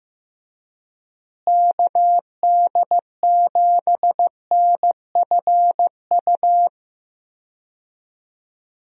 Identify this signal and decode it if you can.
Morse code, heard as a keyed tone, about 15 wpm, 700 Hz. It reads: KD7NFU
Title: USA Call Signs - 6 Characters - Repeated 2x - 15wpm